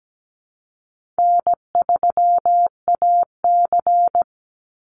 Morse code, heard as a keyed tone, about 17 wpm, 700 Hz. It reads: N3AC
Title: Call Signs Repeated 2x - 17wpm